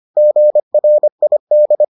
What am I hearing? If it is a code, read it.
Morse code, heard as a keyed tone, about 25 wpm, 600 Hz. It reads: GRID